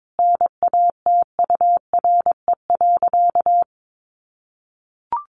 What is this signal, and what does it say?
Morse code, heard as a keyed tone, about 22 wpm, 700 Hz. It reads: NATURE.
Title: Sentences from Top 700 Words 22wpm